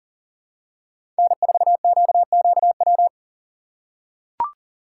Morse code, heard as a keyed tone, about 38 wpm, 700 Hz. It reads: N4QQW
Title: USA Call Signs - 5 Characters - 4x Character Spacing - 40wpm